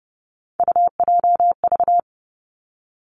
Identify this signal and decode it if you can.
Morse code, heard as a keyed tone, about 30 wpm, 700 Hz. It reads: UJV
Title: Three Challenging Letter-Number Combinations 30wpm